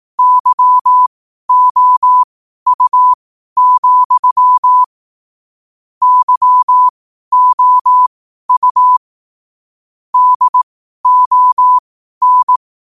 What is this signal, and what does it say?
Morse code, heard as a keyed tone, about 18 wpm, spaced as for 12 wpm, 1000 Hz. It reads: YOU, YOU DON
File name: qotdfiles / 12wpm